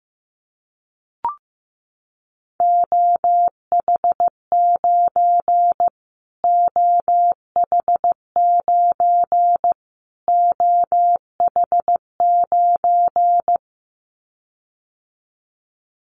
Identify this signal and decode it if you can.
Morse code, heard as a keyed tone, about 15 wpm, 700 Hz. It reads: OH9 OH9 OH9
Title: Callsign Prefixes for Top 25 DXCC - Repeated 3x - 15wpm